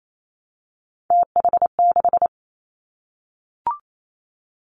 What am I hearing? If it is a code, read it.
Morse code, heard as a keyed tone, about 28 wpm, 690 Hz. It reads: TH6
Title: Three Challenging Letter-Number Combinations 28wpm